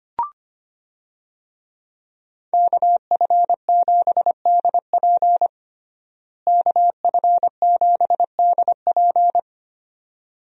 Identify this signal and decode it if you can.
Morse code, heard as a keyed tone, about 25 wpm, 700 Hz. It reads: KF7DP KF7DP
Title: USA Call Signs - 5 Characters - Repeated 2x - 25wpm